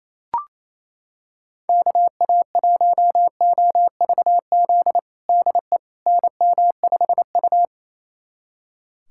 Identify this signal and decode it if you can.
Morse code, heard as a keyed tone, about 28 wpm, 700 Hz. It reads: KA1OVZ DE NM5U
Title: USA Callsign DE Callsign 28wpm